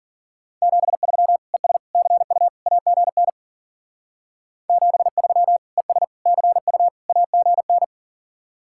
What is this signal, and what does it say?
Morse code, heard as a keyed tone, about 48 wpm, 700 Hz. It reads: 73 ES CU AGN 73 ES CU AGN
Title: Speed-Racing Common QSO Phrases - 40wpm